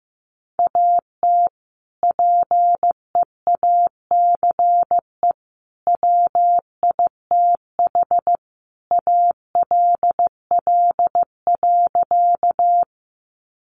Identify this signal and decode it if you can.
Morse code, heard as a keyed tone, about 15 wpm, 700 Hz. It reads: AT PEACE WITH ALL.
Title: Sentences from Top 1500 Words 15wpm